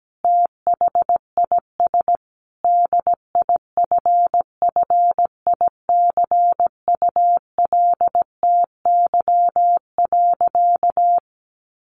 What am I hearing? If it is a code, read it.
Morse code, heard as a keyed tone, about 17 wpm, 700 Hz. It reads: THIS DIFFICULTY.
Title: Sentences from Top 1500 Words 17wpm